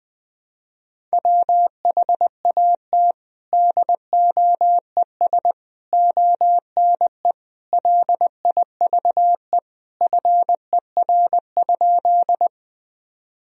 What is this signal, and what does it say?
Morse code, heard as a keyed tone, about 20 wpm, 700 Hz. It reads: WHAT DOES ONE LIVE FER?
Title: Sentences from Top 200 Words and Abbreviations 20wpm